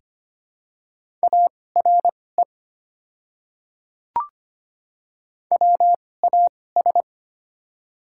Morse code, heard as a keyed tone, about 25 wpm, spaced as for 18 wpm, 700 Hz. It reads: ARE WAS
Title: 2x Character Spacing - Top 100 Words - 25wpm